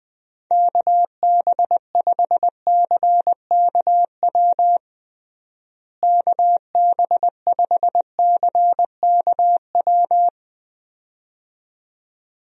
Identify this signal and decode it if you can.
Morse code, heard as a keyed tone, about 20 wpm, 700 Hz. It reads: KB5CKW KB5CKW
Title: USA Call Signs - 6 Characters - Repeated 2x - 20wpm